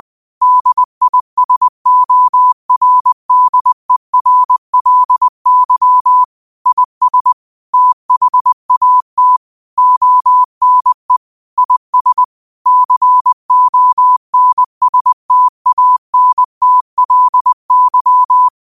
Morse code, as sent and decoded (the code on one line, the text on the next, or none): -.. .. ... --- .-. -.. .-. .-.. -.-- / .. ... / - .... .- - / --- -. / .. ... / -.-. --- -. ... - .- -. - .-.. -.--
DISORDERLY IS THAT ONE IS CONSTANTLY